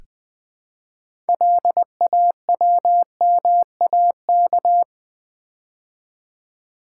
Morse code, heard as a keyed tone, about 20 wpm, 700 Hz. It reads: LAWMAK
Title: Build A Word - 2 Letters At A Time - 8-Letter Words - 20wpm